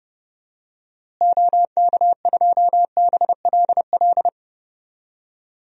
Morse code, heard as a keyed tone, about 31 wpm, 700 Hz. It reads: OK2BLL
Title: DXCC-100 Call Signs - 6 Characters - Repeated 3x - 30wpm